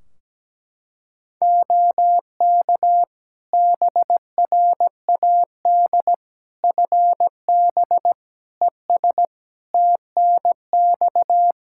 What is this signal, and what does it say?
Morse code, heard as a keyed tone, about 17 wpm, 700 Hz. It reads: OK BRAD FB ES TNX
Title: K1USN SST QSO Practice 17wpm